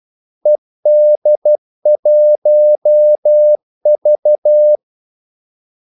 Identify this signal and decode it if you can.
Morse code, heard as a keyed tone, about 12 wpm, 600 Hz. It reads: ED1V
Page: 2 ekstra ordmellemrum mellem kaldesignalerne